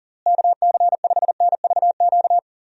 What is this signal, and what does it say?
Morse code, heard as a keyed tone, about 40 wpm, 700 Hz. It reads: KC5NVQ
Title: USA Call Signs - 6 Characters - Repeated 2x - 40wpm